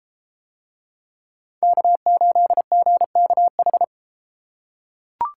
Morse code, heard as a keyed tone, about 32 wpm, 700 Hz. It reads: K8GKH